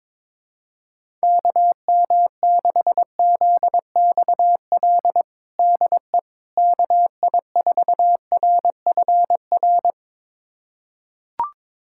Morse code, heard as a keyed tone, about 22 wpm, 700 Hz. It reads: KM6ZXL DE KI4RFR
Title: USA Callsign DE Callsign 22wpm